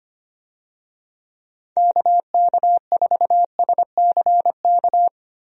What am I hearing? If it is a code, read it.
Morse code, heard as a keyed tone, about 25 wpm, 700 Hz. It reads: KK4SCK
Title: USA Call Signs - 6 Characters - 25wpm